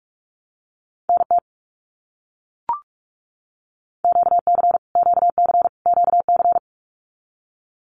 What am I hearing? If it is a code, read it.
Morse code, heard as a keyed tone, about 44 wpm, 700 Hz. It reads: NT QC QC QC
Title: Canadian Provinces and Territories - Repeated 3x - 45wpm